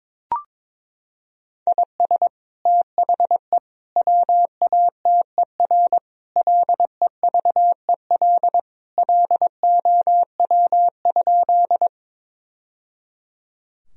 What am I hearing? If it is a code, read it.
Morse code, heard as a keyed tone, about 22 wpm, 700 Hz. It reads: IS THE WATER LEVEL LOW?
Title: Sentences from Top 1000 Words - Review Part 4 - 22wpm